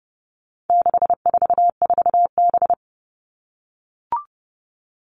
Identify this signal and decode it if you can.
Morse code, heard as a keyed tone, about 28 wpm, 700 Hz. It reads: 644B
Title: All Cut Numbers - 4-Digit 30wpm